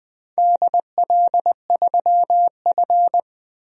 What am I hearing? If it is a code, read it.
Morse code, heard as a keyed tone, about 19 wpm, 700 Hz. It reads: DL3F
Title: DXCC-100 Call Signs - 4 Characters - Repeated 3x - 20wpm